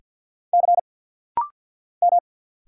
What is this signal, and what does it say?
Morse code, heard as a keyed tone, about 50 wpm, 700 Hz. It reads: C M
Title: Single Letter-Number Alt - Rapid-Fire 50wpm